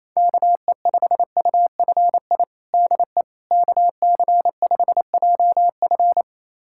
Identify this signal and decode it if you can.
Morse code, heard as a keyed tone, about 28 wpm, 700 Hz. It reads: KE5UFI DE KC5JF
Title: USA Callsign DE Callsign 28wpm